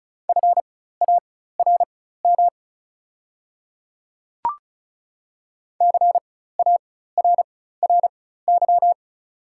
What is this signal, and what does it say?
Morse code, heard as a keyed tone, about 36 wpm, spaced as for 16 wpm, 690 Hz. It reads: FARM CARRY
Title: 4x Character Spacing - Top 500 Words - 35wpm